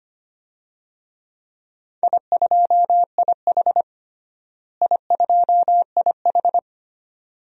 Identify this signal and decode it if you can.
Morse code, heard as a keyed tone, about 25 wpm, 700 Hz. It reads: I2IH I2IH
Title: DXCC-100 Call Signs - 4 Characters - Repeated 2x - 25wpm